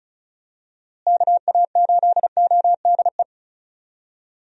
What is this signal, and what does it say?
Morse code, heard as a keyed tone, about 34 wpm, 700 Hz. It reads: KA8ODE
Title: USA Call Signs - 6 Characters - Repeated 2x - 35wpm